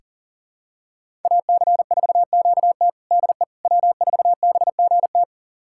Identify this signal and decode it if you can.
Morse code, heard as a keyed tone, about 40 wpm, 700 Hz. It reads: AC4QT DE W4BGT
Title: USA Callsign DE Callsign - 5 Characters - 40wpm